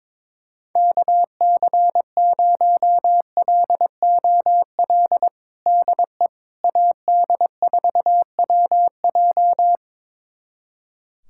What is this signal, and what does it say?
Morse code, heard as a keyed tone, about 22 wpm, 700 Hz. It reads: KC0LOL DE AD4WJ
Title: USA Callsign DE Callsign 22wpm